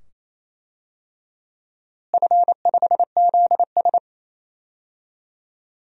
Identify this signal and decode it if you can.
Morse code, heard as a keyed tone, about 27 wpm, 700 Hz. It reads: F5ZS
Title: DXCC-100 Call Signs - 4 Characters - Repeated 2x - 28wpm